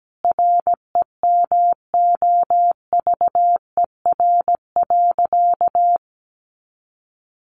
Morse code, heard as a keyed tone, about 17 wpm, 700 Hz. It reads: REMOVER.
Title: Additional Easy Phrases 17wpm